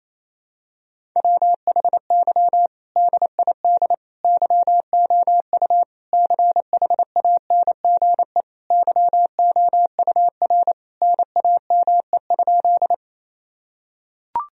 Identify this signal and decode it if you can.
Morse code, heard as a keyed tone, about 28 wpm, 700 Hz. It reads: WHY DID YOU CHANGE YOUR NAME?
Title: Sentences from Top 1000 Words - Review Part 1 - 28wpm